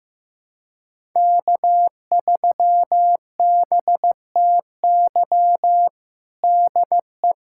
Morse code, heard as a keyed tone, about 15 wpm, 700 Hz. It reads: K3BTY DE
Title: USA Callsign DE Callsign - 5 Characters - 15wpm